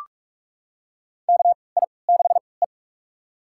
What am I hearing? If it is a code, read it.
Morse code, heard as a keyed tone, about 44 wpm, spaced as for 25 wpm, 690 Hz. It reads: KI6E
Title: USA Call Signs - 4 Characters - 3x Character Spacing - 45wpm